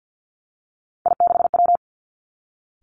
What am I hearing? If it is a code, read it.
Morse code, heard as a keyed tone, about 50 wpm, 700 Hz. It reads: I6W
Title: Three Challenging Letter-Number Combinations 50wpm